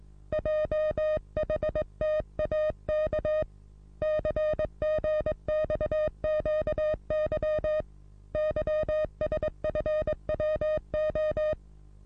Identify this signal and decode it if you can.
Morse code, heard as a keyed tone, about 17 wpm, 620 Hz. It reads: JHTAK CGXQY YSFWO
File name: repos / Morsekurs / DJ4UF Morsen27